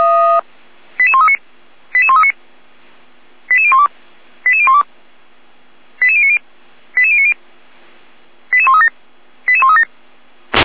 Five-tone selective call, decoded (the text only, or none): 88139 88139 88012 88012 88090 88090 88137 88137